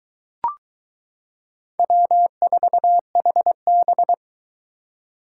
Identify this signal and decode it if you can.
Morse code, heard as a keyed tone, about 22 wpm, 700 Hz. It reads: W4HB